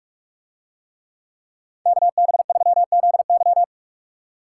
Morse code, heard as a keyed tone, about 46 wpm, 700 Hz. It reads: KB3ZY